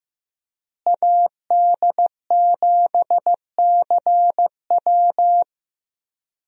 Morse code, heard as a keyed tone, about 15 wpm, 700 Hz. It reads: AD7CW
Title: USA Call Signs - 5 Characters - Repeated 3x - 15wpm